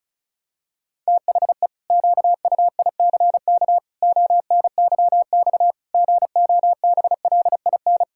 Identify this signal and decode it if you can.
Morse code, heard as a keyed tone, about 34 wpm, 700 Hz. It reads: THE QUICK ONYX GOBLIN
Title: Pangram Challenge v2 - 35wpm